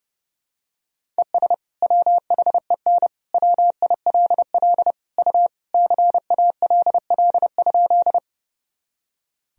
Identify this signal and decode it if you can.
Morse code, heard as a keyed tone, about 30 wpm, 700 Hz. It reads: ES WHEN WILL U CALL?
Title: Sentences from Top 100 Words and Abbreviations 30wpm